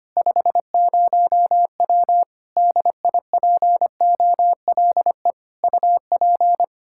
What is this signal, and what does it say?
Morse code, heard as a keyed tone, about 25 wpm, 700 Hz. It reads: 50W DIPOLE UP
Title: K1USN SST QSO Practice 25wpm